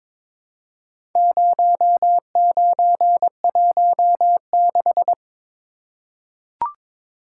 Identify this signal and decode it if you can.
Morse code, heard as a keyed tone, about 22 wpm, 700 Hz. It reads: 0916
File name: practice / 22wpm / Four Letter-Number Combinations - 22wpm